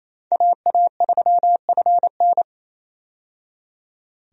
Morse code, heard as a keyed tone, about 28 wpm, 700 Hz. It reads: AA3FN
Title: USA Call Signs - 5 Characters - 28wpm